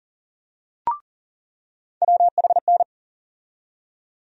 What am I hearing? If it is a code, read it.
Morse code, heard as a keyed tone, about 40 wpm, 700 Hz. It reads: WHN
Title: USA Call Sign Postfixes - 3 Letters 40wpm